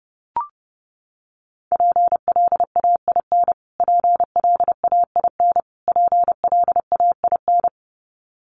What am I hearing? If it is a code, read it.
Morse code, heard as a keyed tone, about 30 wpm, 700 Hz. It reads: PLAIN PLAIN PLAIN
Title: Top 500 Words Repeated 30wpm